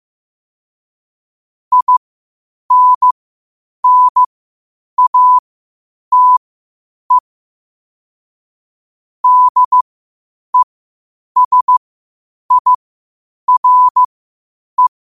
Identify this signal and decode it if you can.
Morse code, heard as a keyed tone, about 15 wpm, spaced as for 8 wpm, 1000 Hz. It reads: INNATE DESIRE
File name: qotdfiles / 8wpm